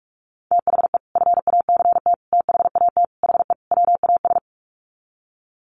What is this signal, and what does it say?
Morse code, heard as a keyed tone, about 44 wpm, 700 Hz. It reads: THE FACT THAT HE WAS
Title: Sets of 5 Words 45wpm